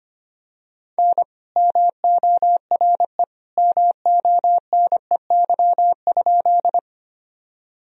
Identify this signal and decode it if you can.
Morse code, heard as a keyed tone, about 25 wpm, 700 Hz. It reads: N MORE MONEY?
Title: Sentences from Top 400 Words and Abbreviations 25wpm